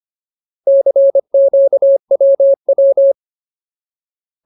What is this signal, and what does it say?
Morse code, heard as a keyed tone, about 25 wpm, 550 Hz. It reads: CQWW
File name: Ultimate Word List / 25WPM Ultimate